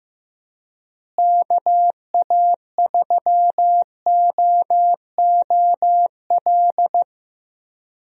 Morse code, heard as a keyed tone, about 15 wpm, 700 Hz. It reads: KA3OOL
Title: USA Call Signs - 6 Characters - Repeated 3x - 15wpm